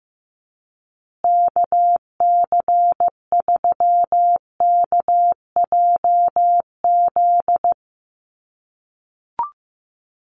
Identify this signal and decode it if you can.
Morse code, heard as a keyed tone, about 15 wpm, 700 Hz. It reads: KC3KJZ
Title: USA Call Signs Repeated 2x 15wpm